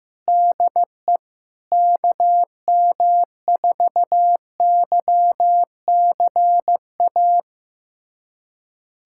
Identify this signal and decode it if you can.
Morse code, heard as a keyed tone, about 15 wpm, 700 Hz. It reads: DE KM4YCA
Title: USA Callsign DE Callsign - 6 Characters - 15wpm